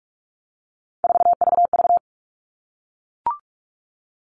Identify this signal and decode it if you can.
Morse code, heard as a keyed tone, about 44 wpm, 700 Hz. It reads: 4VV
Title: All Cut Numbers - 3-Digit 45wpm